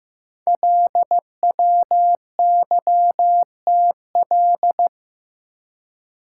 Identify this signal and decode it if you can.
Morse code, heard as a keyed tone, about 15 wpm, 700 Hz. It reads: LWYTL
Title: Code Groups - 5 Letters - 15wpm